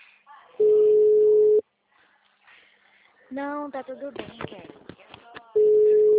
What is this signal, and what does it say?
Ringback tone (Europe, 425 Hz), with speech.
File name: out- -2003- - - .161